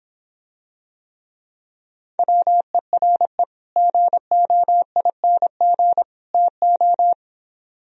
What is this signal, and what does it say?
Morse code, heard as a keyed tone, about 26 wpm, 700 Hz. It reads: WERE GOING TO